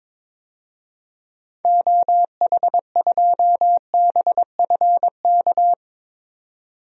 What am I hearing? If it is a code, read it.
Morse code, heard as a keyed tone, about 22 wpm, 700 Hz. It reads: OH2BFK
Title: DXCC-100 Call Signs - 6 Characters - 22wpm